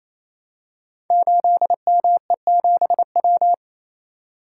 Morse code, heard as a keyed tone, about 27 wpm, 700 Hz. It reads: 8ME7W
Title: Code Groups - 5 Letter:Numbers - 28wpm